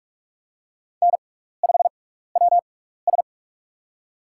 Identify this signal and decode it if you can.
Morse code, heard as a keyed tone, about 42 wpm, spaced as for 15 wpm, 700 Hz. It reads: N5WS
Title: USA Call Signs - 4 Characters - 6x Character Spacing - 45wpm